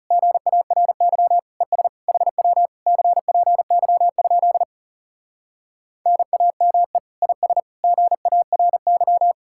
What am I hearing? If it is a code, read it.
Morse code, heard as a keyed tone, about 40 wpm, 700 Hz. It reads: GARY ES HW CPY? NAME IS GARY